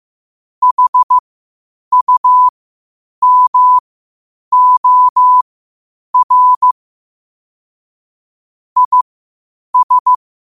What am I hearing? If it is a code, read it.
Morse code, heard as a keyed tone, about 15 wpm, spaced as for 8 wpm, 1000 Hz. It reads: HUMOR IS
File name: qotdfiles / 8wpm